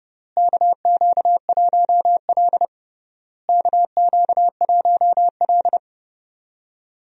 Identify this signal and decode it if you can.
Morse code, heard as a keyed tone, about 32 wpm, 700 Hz. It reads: KQ1L KQ1L
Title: USA Call Signs - 4 Characters - Repeated 3x - 30wpm